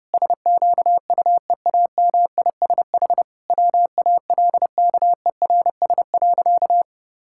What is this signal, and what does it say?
Morse code, heard as a keyed tone, about 31 wpm, 700 Hz. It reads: SQUEAMISH WALKERS.
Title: Pangram Challenge v2 - 30wpm